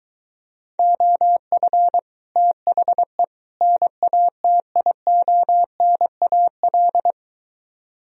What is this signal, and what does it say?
Morse code, heard as a keyed tone, about 22 wpm, 700 Hz. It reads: OF THE NATIONAL